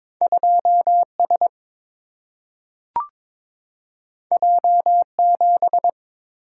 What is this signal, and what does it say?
Morse code, heard as a keyed tone, about 23 wpm, 690 Hz. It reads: 2S J7
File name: practice / 22wpm / Two Letter-Number Combinations - 22wpm